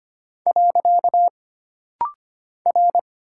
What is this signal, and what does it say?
Morse code, heard as a keyed tone, about 25 wpm, 700 Hz. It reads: . R
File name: Single Character (Letter:Number:Punctuation:Prosign) - Rapid-Fire - 25wpm